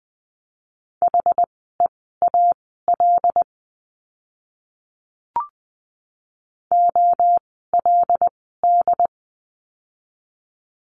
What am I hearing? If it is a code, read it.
Morse code, heard as a keyed tone, about 20 wpm, spaced as for 14 wpm, 700 Hz. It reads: HEAL OLD
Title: 075 Lesson 13f - Letter L - Words 20wpm